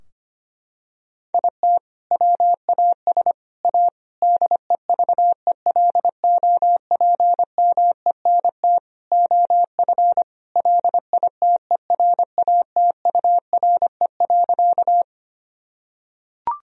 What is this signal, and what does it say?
Morse code, heard as a keyed tone, about 25 wpm, 700 Hz. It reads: IT WAS A DEVELOPMENT OF LITERATURE.
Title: Sentences from Top 1000 Words - Review Part 4 - 25wpm